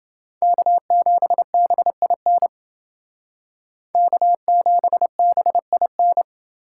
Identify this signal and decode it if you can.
Morse code, heard as a keyed tone, about 29 wpm, 700 Hz. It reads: K7BIN K7BIN